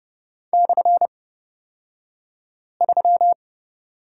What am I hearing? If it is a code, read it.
Morse code, heard as a keyed tone, about 30 wpm, spaced as for 5 wpm, 700 Hz. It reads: /3
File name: Lesson 36d - BT - Two Characters 30wpm